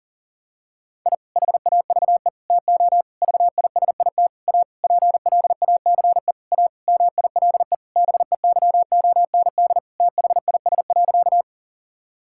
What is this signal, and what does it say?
Morse code, heard as a keyed tone, about 40 wpm, 700 Hz. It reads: I HAVE TO VISIT A PLACE A MILE BEYOND THIS.
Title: Sentences from Top 1000 Words - Review Part 6 - 40wpm